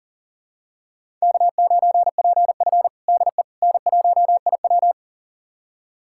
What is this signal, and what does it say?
Morse code, heard as a keyed tone, about 40 wpm, 700 Hz. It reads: K9PF DE N1IW
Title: USA Callsign DE Callsign - 4 Characters - 40wpm